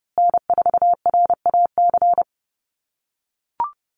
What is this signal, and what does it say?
Morse code, heard as a keyed tone, about 30 wpm, 700 Hz. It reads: N4RAC
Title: USA Call Signs Repeated 2x 30wpm